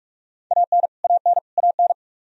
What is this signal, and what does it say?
Morse code, heard as a keyed tone, about 46 wpm, 700 Hz. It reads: AN AN AN
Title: Top 100 Words Repeated 45wpm